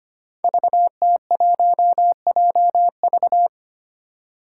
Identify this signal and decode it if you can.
Morse code, heard as a keyed tone, about 25 wpm, 700 Hz. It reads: VT1JV
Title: Five Challenging Letter-Number Combinations 25wpm